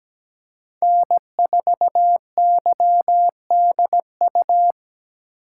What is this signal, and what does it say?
Morse code, heard as a keyed tone, about 17 wpm, 700 Hz. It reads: N4YDU
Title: K1USN SST QSO Practice - 17wpm